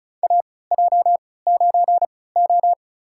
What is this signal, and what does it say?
Morse code, heard as a keyed tone, about 36 wpm, spaced as for 20 wpm, 700 Hz. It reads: AJ9O
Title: USA Call Signs - 4 Characters - 3x Character Spacing - 35wpm